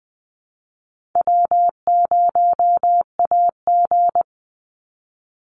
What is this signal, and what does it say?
Morse code, heard as a keyed tone, about 20 wpm, 700 Hz. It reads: W0AG
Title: Easy Call Signs Repeated 3x - 20wpm